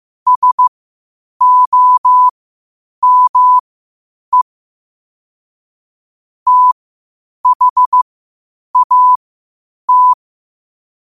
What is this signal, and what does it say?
Morse code, heard as a keyed tone, about 15 wpm, spaced as for 8 wpm, 1000 Hz. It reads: SOME THAT
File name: qotdfiles / 8wpm